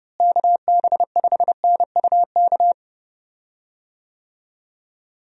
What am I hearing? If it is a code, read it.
Morse code, heard as a keyed tone, about 30 wpm, 700 Hz. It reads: KB5NUK
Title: USA Call Signs - 6 Characters - Repeated 2x - 30wpm